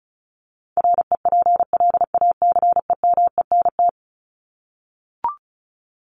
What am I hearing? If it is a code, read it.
Morse code, heard as a keyed tone, about 34 wpm, 700 Hz. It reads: REPLACEMENT
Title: Top 1751-2000 Words 35wpm